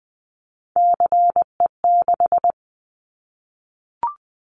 Morse code, heard as a keyed tone, about 20 wpm, 700 Hz. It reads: CE6
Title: Callsign Prefixes for Top 100 DXCC - 20wpm